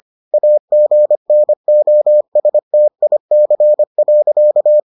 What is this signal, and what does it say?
Morse code, heard as a keyed tone, about 25 wpm, 600 Hz. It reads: AGNOSTIC.